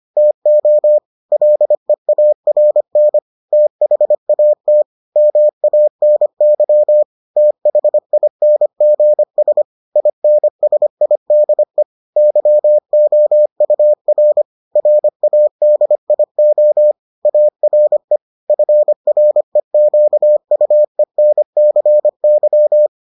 Morse code, heard as a keyed tone, about 25 wpm, 600 Hz. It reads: TO LEARN THAT MANY THINGS INSIDE YOUR RADIO ARE FREQUENCY